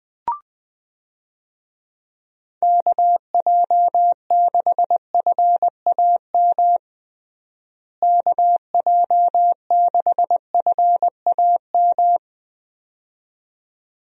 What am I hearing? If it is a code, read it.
Morse code, heard as a keyed tone, about 20 wpm, 700 Hz. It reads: KJ6FAM KJ6FAM
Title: USA Call Signs - 6 Characters - Repeated 2x - 20wpm